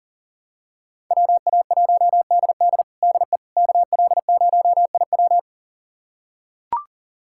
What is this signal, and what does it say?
Morse code, heard as a keyed tone, about 40 wpm, 700 Hz. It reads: WA1DD DE KL0IW
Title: USA Callsign DE Callsign - 5 Characters - 40wpm